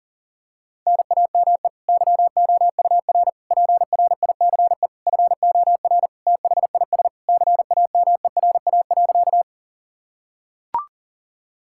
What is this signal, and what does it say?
Morse code, heard as a keyed tone, about 37 wpm, 700 Hz. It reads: NAME YOUR PRICE FOR THIS CAMERA.
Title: Sentences from Top 1000 Words - Review Part 5 - 40wpm